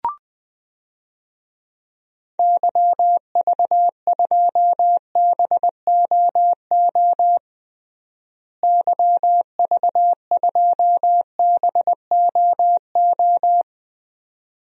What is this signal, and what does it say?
Morse code, heard as a keyed tone, about 20 wpm, 700 Hz. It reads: YV2BOO YV2BOO
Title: DXCC-100 Call Signs - 6 Characters - Repeated 2x - 20wpm